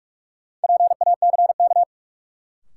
Morse code, heard as a keyed tone, about 46 wpm, 700 Hz. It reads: PACK
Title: Sentences from Top 1300 Words - New Word Drill - 45wpm